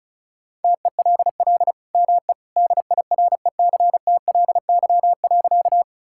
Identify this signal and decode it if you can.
Morse code, heard as a keyed tone, about 36 wpm, 690 Hz. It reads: TELL ME DIRECTLY.
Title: Sentences from Top 1000 Words - Review Part 5 - 35wpm